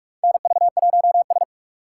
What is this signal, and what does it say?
Morse code, heard as a keyed tone, about 41 wpm, 700 Hz. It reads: NV1S
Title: USA Call Signs - 4 Characters - 2x Character Spacing - 45wpm